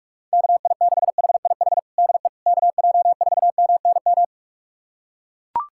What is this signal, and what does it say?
Morse code, heard as a keyed tone, about 41 wpm, 700 Hz. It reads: KI6HIH DE KJ4MNM
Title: USA Callsign DE Callsign 45wpm